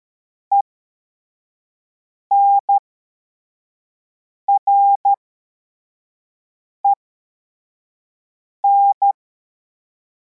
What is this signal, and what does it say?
Morse code, heard as a keyed tone, about 13 wpm, spaced as for 4 wpm, 800 Hz. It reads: ENREN